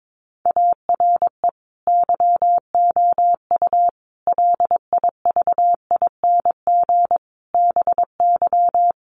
Morse code, heard as a keyed tone, about 22 wpm, 700 Hz. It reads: ARE YOU LIVING BY
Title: Sentences from Top 1100 Words 22wpm